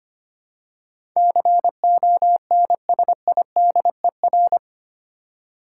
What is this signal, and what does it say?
Morse code, heard as a keyed tone, about 25 wpm, 700 Hz. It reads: CONSIDER
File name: 8 Letter Words - First 3 Letters Given - 25wpm